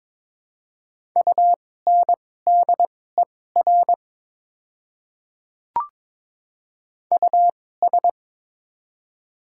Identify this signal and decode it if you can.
Morse code, heard as a keyed tone, about 22 wpm, spaced as for 16 wpm, 700 Hz. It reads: UNDER US
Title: 2x Character Spacing - Top 200 Words - 22wpm